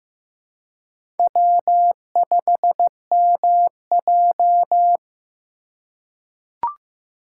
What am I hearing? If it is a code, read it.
Morse code, heard as a keyed tone, about 15 wpm, 700 Hz. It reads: W5MJ
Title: USA Call Signs - 4 Characters - 4x Character Spacing - 15wpm